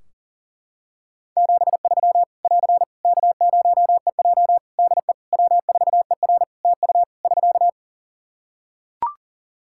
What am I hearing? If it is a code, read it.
Morse code, heard as a keyed tone, about 40 wpm, 700 Hz. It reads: 73 <AR> K0EJ DE W4ER TU <SK>
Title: Practice QSO - Exchange 4 - Ending - 40wpm